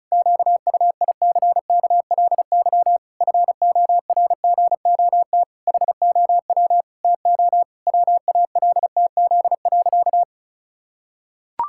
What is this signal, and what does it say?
Morse code, heard as a keyed tone, about 34 wpm, 700 Hz. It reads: QUICKLY FORGOT HOW TO WALTZ.
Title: Pangram Challenge v2 - 35wpm